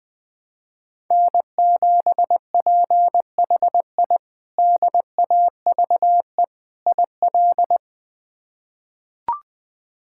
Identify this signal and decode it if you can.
Morse code, heard as a keyed tone, about 20 wpm, 700 Hz. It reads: N7PHI DAVE IL
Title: K1USN SST QSO Practice - 20wpm